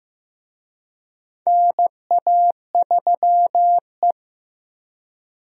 Morse code, heard as a keyed tone, about 15 wpm, 700 Hz. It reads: NA3E
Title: USA Call Signs - 4 Characters - Repeated 3x - 15wpm